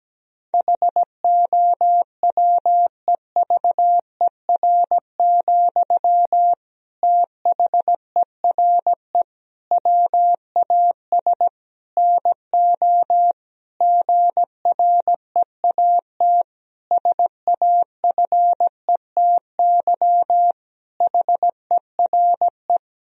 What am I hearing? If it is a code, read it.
Morse code, heard as a keyed tone, about 17 wpm, 700 Hz. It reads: HOWEVER, THERE WAS NO GREAT SAFETY HERE
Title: Sentences from Top 1000 Words - Review Part 4 - 17wpm